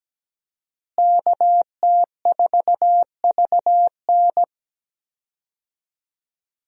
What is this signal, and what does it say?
Morse code, heard as a keyed tone, about 17 wpm, 700 Hz. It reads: KT4VN
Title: USA Call Signs - 5 Characters - Repeated 2x - 17wpm